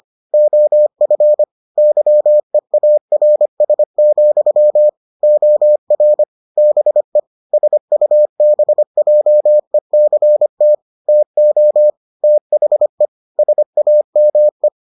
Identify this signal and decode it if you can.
Morse code, heard as a keyed tone, about 25 wpm, 600 Hz. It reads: OF YEARS, OR BE SUBJECT TO THE SAME